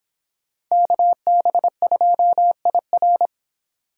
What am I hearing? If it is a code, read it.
Morse code, heard as a keyed tone, about 25 wpm, 700 Hz. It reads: KB2IR